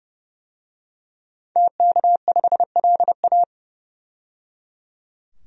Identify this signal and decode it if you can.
Morse code, heard as a keyed tone, about 30 wpm, 700 Hz. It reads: TK5LA
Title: DXCC-100 Call Signs - 5 Characters - 30wpm